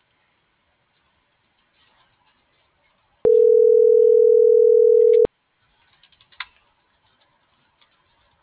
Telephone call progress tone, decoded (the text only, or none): ringback tone